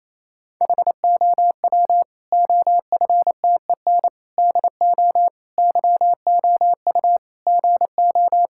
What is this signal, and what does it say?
Morse code, heard as a keyed tone, about 28 wpm, 700 Hz. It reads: HOW OFTEN DO YOU GO
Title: Sentences from Top 1000 Words - Review Part 5 - 28wpm